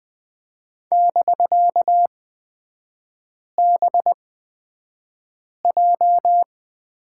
Morse code, heard as a keyed tone, about 20 wpm, spaced as for 5 wpm, 690 Hz. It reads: <BK>BJ
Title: 255 Lesson 41e - Period - Three Characters 20wpm